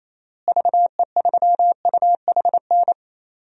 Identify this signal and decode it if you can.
Morse code, heard as a keyed tone, about 28 wpm, 700 Hz. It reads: VE3UHN
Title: DXCC-100 Call Signs - 6 Characters - Repeated 2x - 28wpm